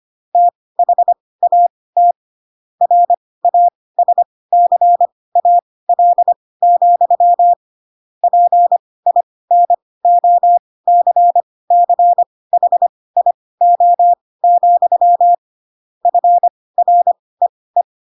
Morse code, heard as a keyed tone, about 25 wpm, spaced as for 18 wpm, 700 Hz. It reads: THAT RASCAL, PINOCCHIO, FREE
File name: Pinocchio / Pinocchio ch4 12 25 ik1hge